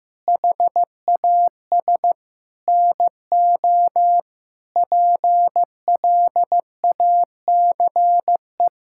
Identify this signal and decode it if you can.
Morse code, heard as a keyed tone, about 15 wpm, 700 Hz. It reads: HAS NO PLACE
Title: Sentences from Top 1600 Words 15wpm